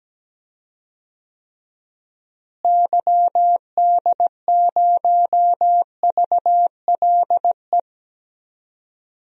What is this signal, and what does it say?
Morse code, heard as a keyed tone, about 17 wpm, 700 Hz. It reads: YD0VLE
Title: DXCC-100 Call Signs - 6 Characters - Repeated 3x - 17wpm